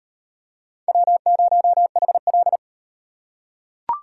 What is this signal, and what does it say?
Morse code, heard as a keyed tone, about 38 wpm, 700 Hz. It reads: W0HL